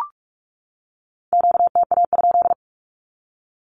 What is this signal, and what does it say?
Morse code, heard as a keyed tone, about 44 wpm, 700 Hz. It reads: QTA?
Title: Q-Codes 45wpm